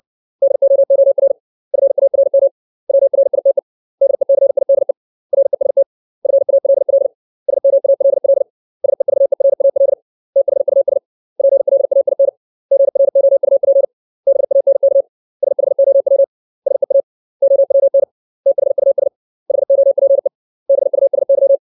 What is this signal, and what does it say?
Morse code, heard as a keyed tone, about 60 wpm, 550 Hz. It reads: DOOR FARM WRITE DECIDE REST FALL SMALL SEVERAL THUS PLAIN GROUP BANK SHOW SEA OWN THUS HOPE BUSY
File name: 632 words / 632 Words 60WPM